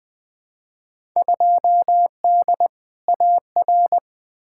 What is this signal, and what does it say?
Morse code, heard as a keyed tone, about 20 wpm, 700 Hz. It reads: 2D AR
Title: Field Day Exchanges Encore - 20wpm